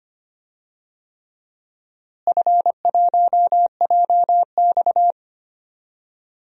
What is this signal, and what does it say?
Morse code, heard as a keyed tone, about 25 wpm, 700 Hz. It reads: F1JX
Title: DXCC-100 Call Signs - 4 Characters - Repeated 3x - 25wpm